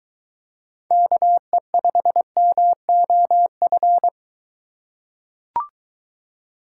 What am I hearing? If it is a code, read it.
Morse code, heard as a keyed tone, about 22 wpm, 700 Hz. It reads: KE5MOF